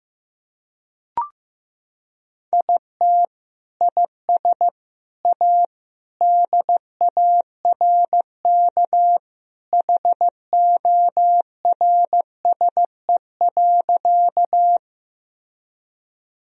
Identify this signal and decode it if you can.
Morse code, heard as a keyed tone, about 15 wpm, 700 Hz. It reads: IT IS A DARK HORSE.
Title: Sentences from Top 1000 Words - Review Part 1 - 15wpm